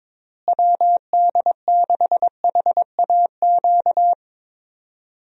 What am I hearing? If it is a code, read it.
Morse code, heard as a keyed tone, about 22 wpm, 700 Hz. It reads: WD6HAQ
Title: USA Call Signs - 6 Characters - Repeated 2x - 22wpm